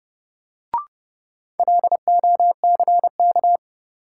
Morse code, heard as a keyed tone, about 30 wpm, 700 Hz. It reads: LOCK
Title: Sentences from Top 1300 Words - New Word Drill Alt - 30wpm